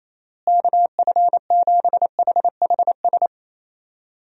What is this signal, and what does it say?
Morse code, heard as a keyed tone, about 28 wpm, 700 Hz. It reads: KF7HHS
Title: USA Call Signs - 6 Characters - Repeated 2x - 28wpm